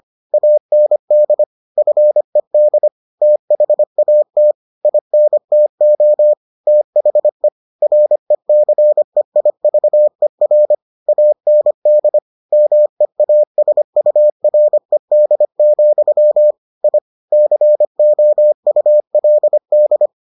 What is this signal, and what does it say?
Morse code, heard as a keyed tone, about 25 wpm, 600 Hz. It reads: AND FED THAT INTO THE RECEIVER AND MEASURED, I COULD